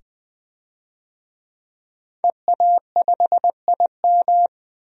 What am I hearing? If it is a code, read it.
Morse code, heard as a keyed tone, about 20 wpm, 700 Hz. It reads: EA5IM
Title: DXCC-100 Call Signs - 5 Characters - 20wpm